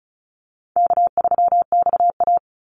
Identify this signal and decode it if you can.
Morse code, heard as a keyed tone, about 34 wpm, 700 Hz. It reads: K3XA
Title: Call Signs Repeated 3x - 35wpm